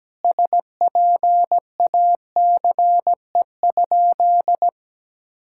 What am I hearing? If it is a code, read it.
Morse code, heard as a keyed tone, about 17 wpm, 700 Hz. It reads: SPACE?
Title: Sentences from Top 1000 Words - Review Part 5 - 17wpm